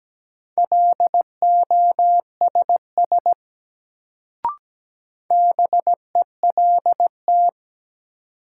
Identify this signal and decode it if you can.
Morse code, heard as a keyed tone, about 17 wpm, 700 Hz. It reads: LOSS BELT
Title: Sentences from Top 1400 Words - New Word Drill - 17wpm